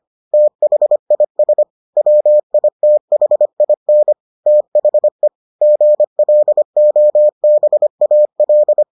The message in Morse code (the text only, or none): THIS WITHIN THE GLOBAL